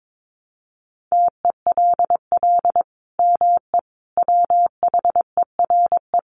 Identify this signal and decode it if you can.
Morse code, heard as a keyed tone, about 22 wpm, 700 Hz. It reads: TELL ME WHERE
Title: Sentences from Top 1500 Words 22wpm